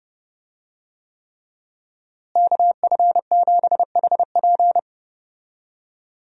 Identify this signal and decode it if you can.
Morse code, heard as a keyed tone, about 32 wpm, 700 Hz. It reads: KF7HP
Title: USA Call Signs - 5 Characters - Repeated 3x - 30wpm